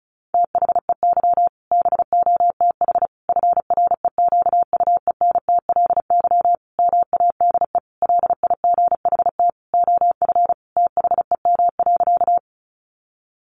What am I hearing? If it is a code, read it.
Morse code, heard as a keyed tone, about 34 wpm, 700 Hz. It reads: THEY BOTH FREQUENTLY MADE LIGHT OF THEM.
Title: Sentences from Top 1300 Words 35wpm